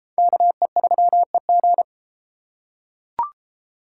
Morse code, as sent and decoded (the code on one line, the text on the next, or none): -.- . ...-- . --.
KE3EG